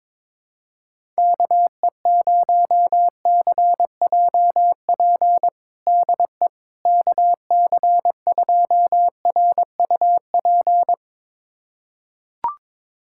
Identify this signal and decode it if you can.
Morse code, heard as a keyed tone, about 22 wpm, 700 Hz. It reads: KE0CJP DE KC2RUP
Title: USA Callsign DE Callsign - 6 Characters - 22wpm